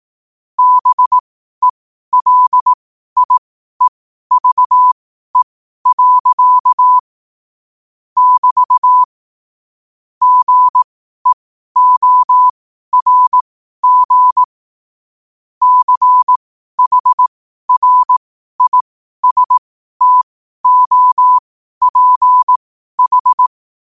Morse code, heard as a keyed tone, about 18 wpm, spaced as for 12 wpm, 1000 Hz. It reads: BELIEVE. <BT> GEORG CHRISTOPH
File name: qotdfiles / 12wpm